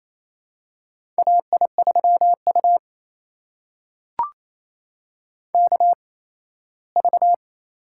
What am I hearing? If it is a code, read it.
Morse code, heard as a keyed tone, about 27 wpm, spaced as for 17 wpm, 700 Hz. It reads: AI3U KV
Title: USA Call Signs - 4 Characters - 8x Character Spacing - 28wpm